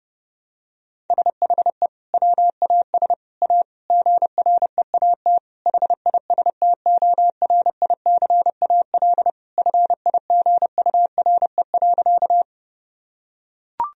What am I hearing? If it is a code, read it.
Morse code, heard as a keyed tone, about 32 wpm, 700 Hz. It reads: SHE WAS A GREAT HISTORICAL FIGURE.
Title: Sentences from Top 1000 Words - Review Part 4 - 30wpm